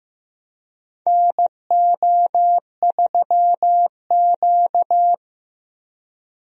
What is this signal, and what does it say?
Morse code, heard as a keyed tone, about 15 wpm, 700 Hz. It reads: NO3Q
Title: USA Call Signs - 4 Characters - 15wpm